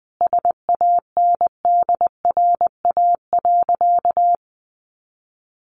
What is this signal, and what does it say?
Morse code, heard as a keyed tone, about 20 wpm, 700 Hz. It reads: SANDRA.
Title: Easy Phrases 20wpm